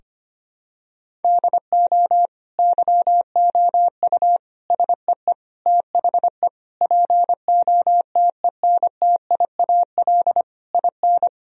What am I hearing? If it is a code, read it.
Morse code, heard as a keyed tone, about 25 wpm, 700 Hz. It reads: DO YOU SEE THE POTENTIAL IN
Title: Sentences from Top 1000 Words - Review Part 4 - 25wpm